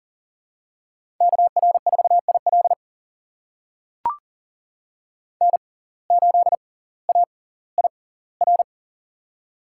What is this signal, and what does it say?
Morse code, heard as a keyed tone, about 38 wpm, spaced as for 20 wpm, 700 Hz. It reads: KR4IL N8AIR
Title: USA Call Signs - 5 Characters - 6x Character Spacing - 40wpm